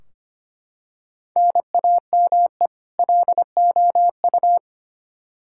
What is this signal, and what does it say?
Morse code, heard as a keyed tone, about 25 wpm, 700 Hz. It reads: NAME LOU
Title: Practice QSO - Exchange 1 - Beginning - 25wpm